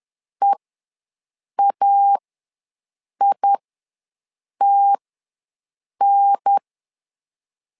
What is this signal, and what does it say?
Morse code, heard as a keyed tone, about 10 wpm, spaced as for 6 wpm, 800 Hz. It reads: EAITN
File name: LEZIONI / 2 UHDN / 2 UHDN ws 10wpm 2044